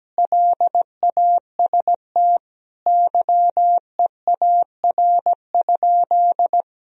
Morse code, heard as a keyed tone, about 17 wpm, 700 Hz. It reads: LAST YEAR?
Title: Sentences from Top 1000 Words - Review Part 4 - 17wpm